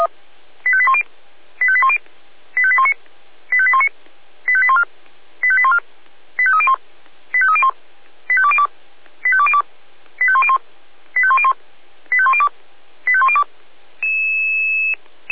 Five-tone selective call (ZVEI), heard as quick runs of five tones, tones of about 80 ms. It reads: 85810 85810 85819 85819 85814 85814 85301 85301 85202 85202 85101 85101 85102 85102